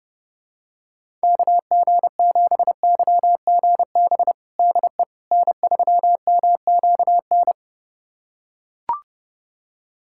Morse code, heard as a keyed tone, about 30 wpm, 700 Hz. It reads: KG7YGB DE N3MQN
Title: USA Callsign DE Callsign 30wpm